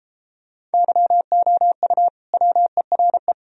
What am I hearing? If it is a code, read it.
Morse code, heard as a keyed tone, about 32 wpm, 700 Hz. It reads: YOU WERE